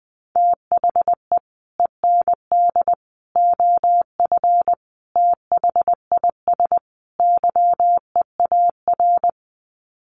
Morse code, heard as a keyed tone, about 20 wpm, 700 Hz. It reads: THE END OF THIS YEAR
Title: Sets of 5 Words Encore 20wpm